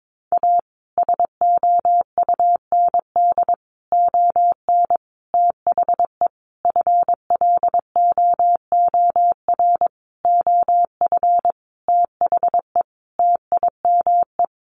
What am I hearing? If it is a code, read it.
Morse code, heard as a keyed tone, about 22 wpm, 700 Hz. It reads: A SOUND ON THE FLOOR OF THE TIME
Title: A Sound Of Thunder 22wpm